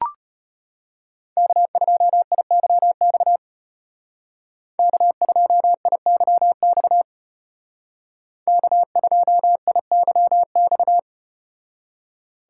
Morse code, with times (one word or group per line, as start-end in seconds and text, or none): 1.4-3.4 K2IYX
4.8-7.0 K2IYX
8.5-11.0 K2IYX